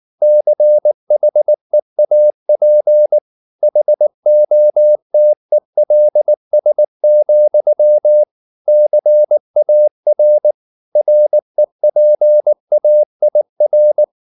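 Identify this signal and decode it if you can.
Morse code, heard as a keyed tone, about 19 wpm, 600 Hz. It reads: CHEAP HOTELS, CAR REPAIR
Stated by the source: Morse code at 19 wpm from CW med Gnister, reading Ken Follett's novel Never in English.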